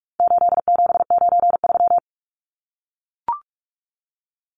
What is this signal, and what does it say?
Morse code, heard as a keyed tone, about 44 wpm, 700 Hz. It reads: 8793
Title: Numbers - 4-Digit 45wpm